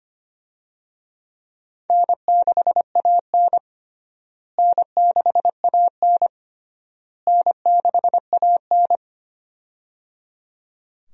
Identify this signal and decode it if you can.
Morse code, heard as a keyed tone, about 25 wpm, 700 Hz. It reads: N6AN N6AN N6AN
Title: USA Call Signs - 4 Characters - Repeated 3x - 25wpm